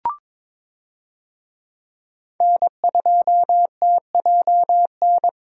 Morse code, heard as a keyed tone, about 21 wpm, 700 Hz. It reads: N2TJN
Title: USA Call Signs - 5 Characters - 22wpm